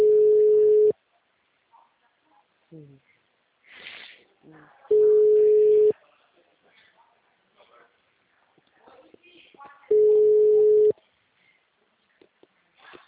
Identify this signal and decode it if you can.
Ringback tone (Europe, 425 Hz), with speech.